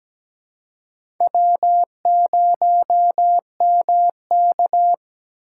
Morse code, heard as a keyed tone, about 17 wpm, 700 Hz. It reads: W0MK
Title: USA Call Signs - 4 Characters - 6x Character Spacing - 17wpm